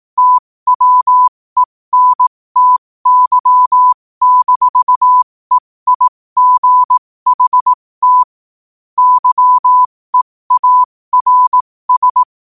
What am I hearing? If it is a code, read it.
Morse code, heard as a keyed tone, about 18 wpm, spaced as for 15 wpm, 1000 Hz. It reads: TWENTY-EIGHT YEARS
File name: qotdfiles / 15wpm